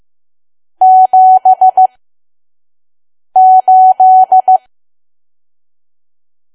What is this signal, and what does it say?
Morse code, heard as a keyed tone, about 15 wpm, spaced as for 5 wpm, 750 Hz. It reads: 78
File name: Archive / 5 WPM files / PM